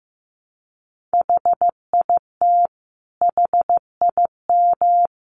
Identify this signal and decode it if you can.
Morse code, heard as a keyed tone, about 14 wpm, 700 Hz. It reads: HIT HIM
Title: Sentences from Top 1200 Words 15wpm